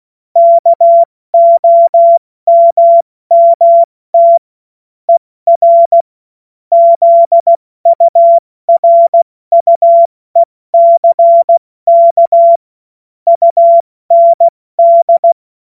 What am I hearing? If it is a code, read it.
Morse code, heard as a keyed tone, about 16 wpm, 680 Hz. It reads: KOMMT ER ZURUECK UND
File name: Morsekurs / Witze / Witz 21. Dezember 2021, 70 F80